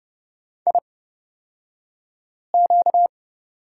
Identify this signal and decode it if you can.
Morse code, heard as a keyed tone, about 30 wpm, spaced as for 5 wpm, 700 Hz. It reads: IQ
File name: Lesson 38d - Letter q - Two Characters 30wpm